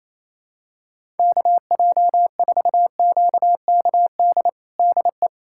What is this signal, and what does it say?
Morse code, heard as a keyed tone, about 28 wpm, 700 Hz. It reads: KJ4QKD DE
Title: USA Callsign DE Callsign - 6 Characters - 28wpm